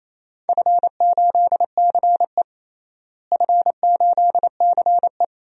Morse code, heard as a keyed tone, about 27 wpm, 700 Hz. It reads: F8CE F8CE
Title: DXCC-100 Call Signs - 4 Characters - Repeated 3x - 28wpm